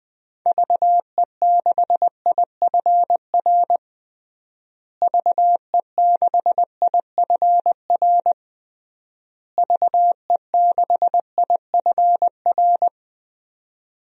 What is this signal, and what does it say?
Morse code, heard as a keyed tone, about 20 wpm, 700 Hz. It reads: VE6IFR VE6IFR VE6IFR
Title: DXCC-100 Call Signs - 6 Characters - Repeated 3x - 20wpm